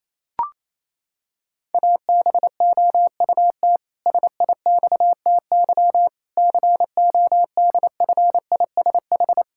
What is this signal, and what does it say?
Morse code, heard as a keyed tone, about 28 wpm, 700 Hz. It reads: ABOUT SIXTY CODFISH
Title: Pangram Challenge v2 - 28wpm